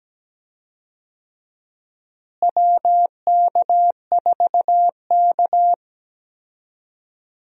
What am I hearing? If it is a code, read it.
Morse code, heard as a keyed tone, about 17 wpm, 700 Hz. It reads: WK4K
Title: USA Call Signs - 4 Characters - Repeated 3x - 17wpm